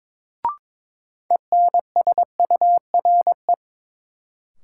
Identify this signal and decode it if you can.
Morse code, heard as a keyed tone, about 22 wpm, 700 Hz. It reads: ENSURE
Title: Sentences from Top 1100 Words - New Word Drill - 22wpm